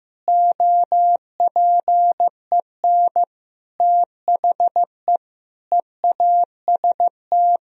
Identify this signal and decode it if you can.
Morse code, heard as a keyed tone, about 15 wpm, 700 Hz. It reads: OPEN THE EAST
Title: Sentences from Top 1000 Words - Review Part 3 - 15wpm